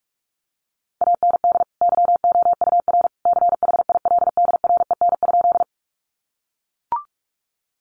Morse code, heard as a keyed tone, about 44 wpm, 700 Hz. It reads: AND YOUR CHILDREN?
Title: Sentences from Top 300 Words 45wpm